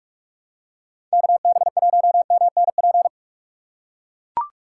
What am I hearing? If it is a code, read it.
Morse code, heard as a keyed tone, about 43 wpm, 700 Hz. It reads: KB1MNP